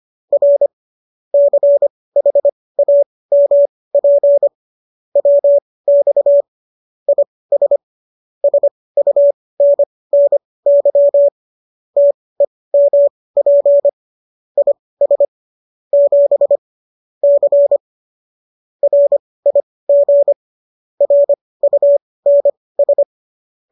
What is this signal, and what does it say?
Morse code, heard as a keyed tone, about 25 wpm, spaced as for 18 wpm, 570 Hz. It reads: R CHAMP WX IS SUNNY TEMP IS 7 C RIG RUNS